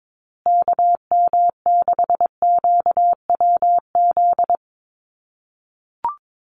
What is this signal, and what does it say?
Morse code, heard as a keyed tone, about 22 wpm, 700 Hz. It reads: KM6QWZ
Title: USA Call Signs Repeated 2x 22wpm